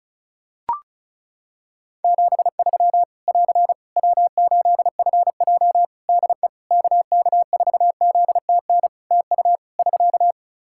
Morse code, heard as a keyed tone, about 36 wpm, 700 Hz. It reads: 73 <AR> W8FJ DE KK4ZTN TU <SK>
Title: Practice QSO - Exchange 4 - Ending - 35wpm